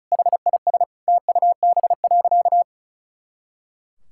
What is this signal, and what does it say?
Morse code, heard as a keyed tone, about 34 wpm, 700 Hz. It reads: HIS TUB.
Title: Pangram Challenge v2 - 35wpm